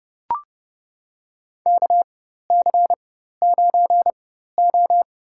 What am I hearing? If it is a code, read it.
Morse code, heard as a keyed tone, about 32 wpm, spaced as for 14 wpm, 700 Hz. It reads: KC9O
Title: USA Call Signs - 4x Character Spacing - 30wpm